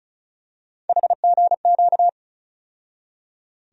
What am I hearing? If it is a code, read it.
Morse code, heard as a keyed tone, about 36 wpm, 700 Hz. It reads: HGQ
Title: USA Call Sign Postfixes - 3 Letters 35wpm